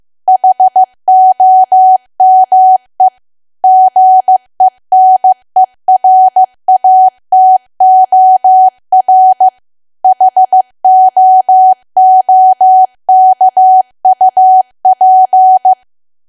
Morse code, heard as a keyed tone, about 15 wpm, 750 Hz. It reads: HOME GENERATOR HOOKUP
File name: cwa / W1AW-Level2 / 15WPM